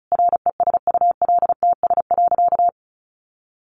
Morse code, heard as a keyed tone, about 34 wpm, 700 Hz. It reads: RESULTS.
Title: A Sound Of Thunder 35wpm